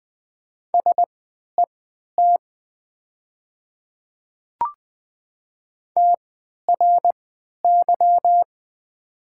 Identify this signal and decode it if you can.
Morse code, heard as a keyed tone, about 20 wpm, spaced as for 11 wpm, 690 Hz. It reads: SET TRY
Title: 3x Character Spacing - Top 300 Words - 20wpm